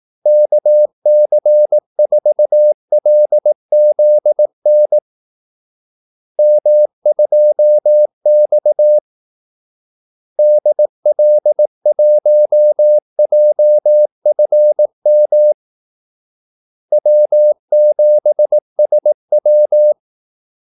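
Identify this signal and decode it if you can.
Morse code, heard as a keyed tone, about 18 wpm, 600 Hz. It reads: KC4LZN M2X DL1JFM W7SW
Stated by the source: Morse call-sign practice at 18 wpm.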